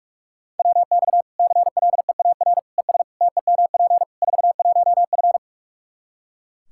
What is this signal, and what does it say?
Morse code, heard as a keyed tone, about 41 wpm, 700 Hz. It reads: WX CLEAR ES TEMP 41F
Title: K1USN SST QSO Practice 45wpm